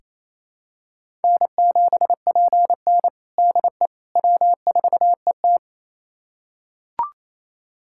Morse code, heard as a keyed tone, about 28 wpm, 700 Hz. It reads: N7PN DE W4ET
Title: USA Callsign DE Callsign - 4 Characters - 28wpm